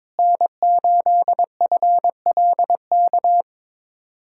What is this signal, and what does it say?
Morse code, heard as a keyed tone, about 22 wpm, 700 Hz. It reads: N8FLK
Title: USA Callsign DE Callsign - 5 Characters - 22wpm